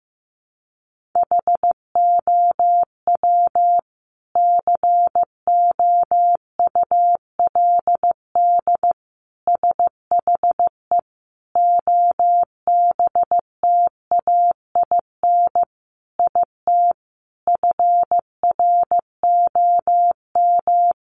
Morse code, heard as a keyed tone, about 15 wpm, 700 Hz. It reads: HOW COULD SHE OBTAIN IT FROM
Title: Sentences from Top 1300 Words 15wpm